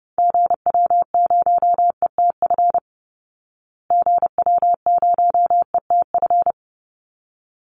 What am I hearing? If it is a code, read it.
Morse code, heard as a keyed tone, about 30 wpm, 700 Hz. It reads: GW0ETF GW0ETF
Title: Call Signs Repeated 3x - 30wpm